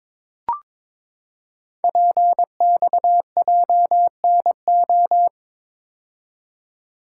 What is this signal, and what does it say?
Morse code, heard as a keyed tone, about 22 wpm, 700 Hz. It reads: PXJNO
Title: Code Groups - 5 Letters - 22wpm